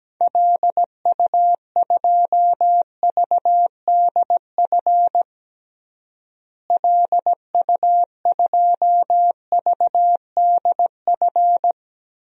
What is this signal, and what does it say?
Morse code, heard as a keyed tone, about 17 wpm, 700 Hz. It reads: LU2VDF LU2VDF
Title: DXCC-100 Call Signs - 6 Characters - Repeated 3x - 17wpm